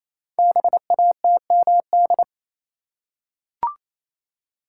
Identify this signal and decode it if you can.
Morse code, heard as a keyed tone, about 27 wpm, 700 Hz. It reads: BATMD
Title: Code Groups - 5 Letters - 28wpm